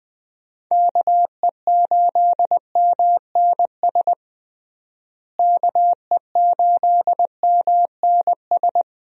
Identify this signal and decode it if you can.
Morse code, heard as a keyed tone, about 20 wpm, 700 Hz. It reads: KE8MNS KE8MNS
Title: USA Call Signs - 6 Characters - Repeated 3x - 20wpm